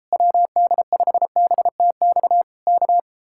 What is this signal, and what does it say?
Morse code, heard as a keyed tone, about 33 wpm, 700 Hz. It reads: WD5BTX K